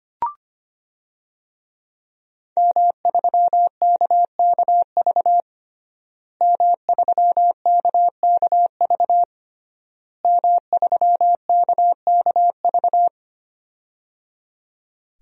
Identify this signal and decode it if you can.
Morse code, heard as a keyed tone, about 25 wpm, 700 Hz. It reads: M3KKV M3KKV M3KKV
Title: DXCC-100 Call Signs - 5 Characters - Repeated 3x - 25wpm